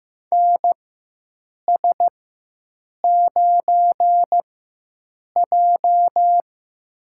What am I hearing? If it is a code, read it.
Morse code, heard as a keyed tone, about 15 wpm, spaced as for 7 wpm, 700 Hz. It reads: NS9J
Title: USA Call Signs - 4x Character Spacing - 15wpm